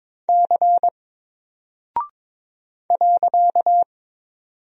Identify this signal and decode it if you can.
Morse code, heard as a keyed tone, about 22 wpm, 700 Hz. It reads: C .
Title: Single Character (Letter:Number:Punctuation:Prosign) 22wpm